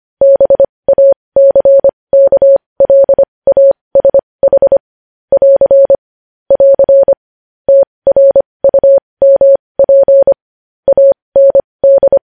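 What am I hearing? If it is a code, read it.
Morse code, heard as a keyed tone, about 25 wpm, spaced as for 20 wpm, 550 Hz. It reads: BACKLASH <AR> <AR> TRUMP AND
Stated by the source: Morse code news headlines sent at 20 wpm.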